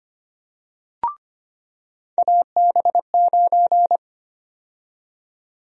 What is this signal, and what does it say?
Morse code, heard as a keyed tone, about 25 wpm, 700 Hz. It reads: AB9
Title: USA Call Sign Prefixes - 25wpm